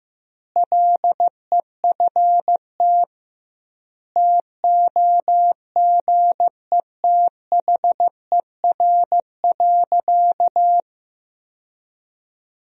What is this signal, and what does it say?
Morse code, heard as a keyed tone, about 15 wpm, 700 Hz. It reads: LEFT TOGETHER.
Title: 2x Word-Spacing - Sentences from Top 400 Words - 15wpm